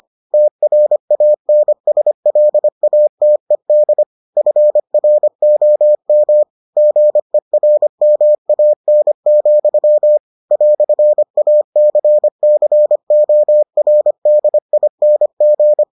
Morse code, heard as a keyed tone, about 25 wpm, 600 Hz. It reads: TRANSLATED FROM GERMAN, "ACCORDING